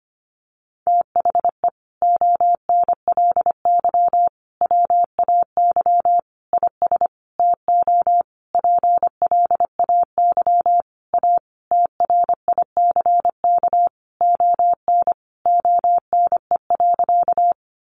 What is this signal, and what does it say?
Morse code, heard as a keyed tone, about 25 wpm, 700 Hz. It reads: THE ONLY WAY IS TO PLAY A TRICK ON ONE.
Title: Sentences from Top 1500 Words 25wpm